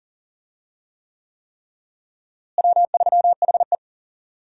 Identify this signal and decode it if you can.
Morse code, heard as a keyed tone, about 40 wpm, 700 Hz. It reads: W3HE
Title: USA Call Signs - 4 Characters - Repeated 3x - 40wpm